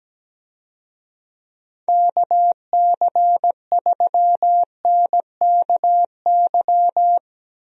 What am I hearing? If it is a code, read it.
Morse code, heard as a keyed tone, about 17 wpm, 700 Hz. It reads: KC3NKY
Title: USA Call Signs - 6 Characters - Repeated 3x - 17wpm